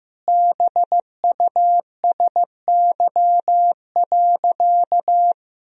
Morse code, heard as a keyed tone, about 15 wpm, 700 Hz. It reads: BUSY.
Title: Sentences from Top 1000 Words - Review Part 3 - 15wpm